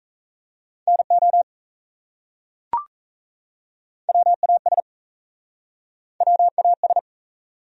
Morse code, heard as a keyed tone, about 37 wpm, 700 Hz. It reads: NO WAS WAS